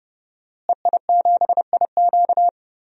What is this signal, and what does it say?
Morse code, heard as a keyed tone, about 31 wpm, 700 Hz. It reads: EI7IQ
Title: DXCC-100 Call Signs - 5 Characters - Repeated 3x - 30wpm